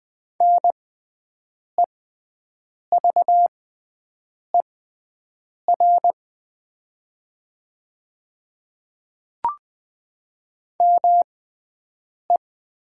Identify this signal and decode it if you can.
Morse code, heard as a keyed tone, about 20 wpm, spaced as for 7 wpm, 700 Hz. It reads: NEVER ME
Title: 6x Character Spacing - Top 300 Words - 20wpm